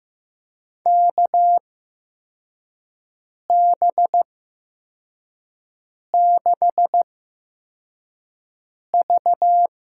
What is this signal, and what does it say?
Morse code, heard as a keyed tone, about 15 wpm, spaced as for 4 wpm, 700 Hz. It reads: KB6V
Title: USA Call Signs - 8x Character Spacing - 15wpm